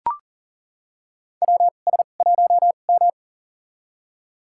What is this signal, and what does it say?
Morse code, heard as a keyed tone, about 40 wpm, spaced as for 29 wpm, 700 Hz. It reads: WS1M
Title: USA Call Signs - 2x Character Spacing - 40wpm